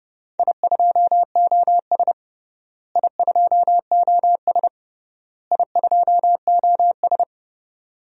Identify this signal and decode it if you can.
Morse code, heard as a keyed tone, about 32 wpm, 700 Hz. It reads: I2OS I2OS I2OS
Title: DXCC-100 Call Signs - 4 Characters - Repeated 3x - 30wpm